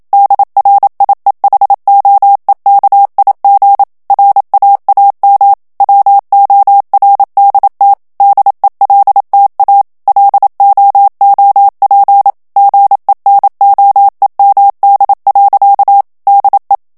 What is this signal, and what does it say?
Morse code, heard as a keyed tone, about 27 wpm, 800 Hz. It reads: DRIEHOEKIG RAAM WORDT DELTA LOOP GENOEMD. DE
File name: HOLLANDAIS texte clair 1 vitesse 26 mots minute